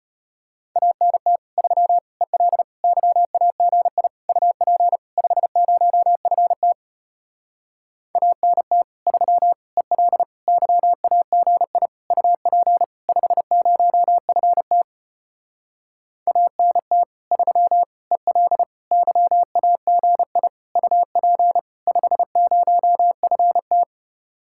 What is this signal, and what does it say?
Morse code, heard as a keyed tone, about 33 wpm, 700 Hz. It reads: ANT 3 EL YAGI UP 50FT ANT 3 EL YAGI UP 50FT ANT 3 EL YAGI UP 50FT